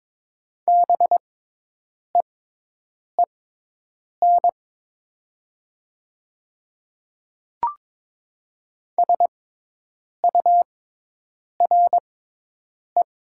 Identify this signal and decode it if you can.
Morse code, heard as a keyed tone, about 22 wpm, spaced as for 8 wpm, 700 Hz. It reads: BEEN SURE
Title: 6x Character Spacing - Top 300 Words - 22wpm